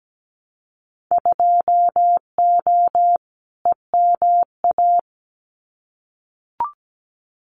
Morse code, heard as a keyed tone, about 17 wpm, 700 Hz. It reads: 2O EMA
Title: Winter Field Day Exchanges Encore 17wpm